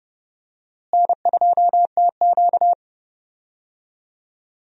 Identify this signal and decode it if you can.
Morse code, heard as a keyed tone, about 31 wpm, 700 Hz. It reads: N2TQ
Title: USA Call Signs - 4 Characters - Repeated 2x - 30wpm